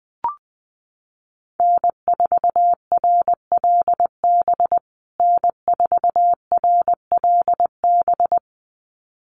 Morse code, heard as a keyed tone, about 20 wpm, 700 Hz. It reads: N4RLB N4RLB
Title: USA Call Signs Repeated 2x 20wpm